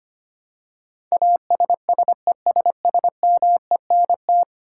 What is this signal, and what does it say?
Morse code, heard as a keyed tone, about 25 wpm, 700 Hz. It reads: ASSESSMENT
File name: Ten Letter Words - 25wpm